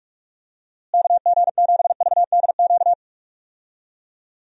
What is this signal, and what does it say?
Morse code, heard as a keyed tone, about 44 wpm, 700 Hz. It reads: KG7VDQ
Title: USA Call Signs - 6 Characters - Repeated 2x - 45wpm